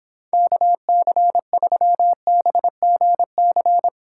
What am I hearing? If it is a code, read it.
Morse code, heard as a keyed tone, about 29 wpm, 700 Hz. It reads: KC3BGC